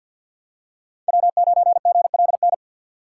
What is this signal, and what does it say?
Morse code, heard as a keyed tone, about 53 wpm, 700 Hz. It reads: W9GLN
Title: USA Call Signs - 5 Characters - 6x Character Spacing - 50wpm